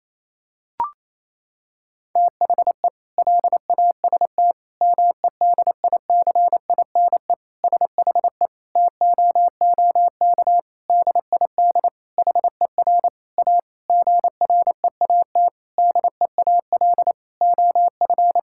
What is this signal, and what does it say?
Morse code, heard as a keyed tone, about 28 wpm, 700 Hz. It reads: THE LAST MEDICINE SHE TOOK DID HER A GREAT DEAL OF
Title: Sentences from Top 1000 Words - Review Part 6 - 28wpm